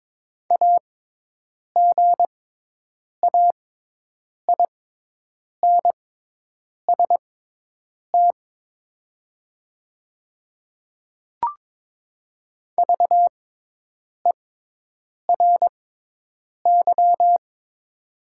Morse code, heard as a keyed tone, about 22 wpm, spaced as for 8 wpm, 700 Hz. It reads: AGAINST VERY
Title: 6x Character Spacing - Top 500 Words - 22wpm